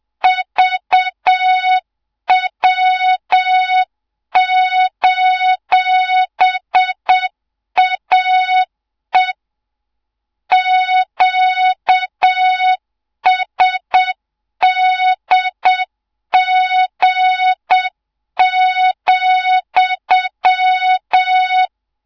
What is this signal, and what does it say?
Morse code, heard as a keyed tone, about 7 wpm, 3700 Hz, with noise. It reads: VW:AE QSDG,
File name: Morze-HA2MN / hang / 24 GYAKORLAT